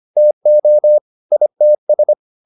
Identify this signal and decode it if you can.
Morse code, heard as a keyed tone, about 25 wpm, 600 Hz. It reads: TO ITS
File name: the old man and the sea part 1 25wpm